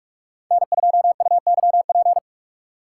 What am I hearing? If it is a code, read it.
Morse code, heard as a keyed tone, about 45 wpm, 700 Hz. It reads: N2UYP
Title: USA Call Signs - 5 Characters - 2x Character Spacing - 45wpm